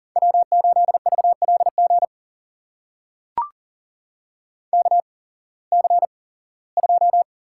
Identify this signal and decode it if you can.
Morse code, heard as a keyed tone, about 40 wpm, spaced as for 21 wpm, 700 Hz. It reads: W8VLG KC2
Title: USA Call Signs - 8x Character Spacing - 40wpm